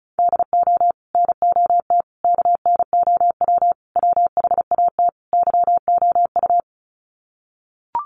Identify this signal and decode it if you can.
Morse code, heard as a keyed tone, about 34 wpm, 700 Hz. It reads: DO NOT KNOW WHAT YOU
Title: Sets of 5 Words 35wpm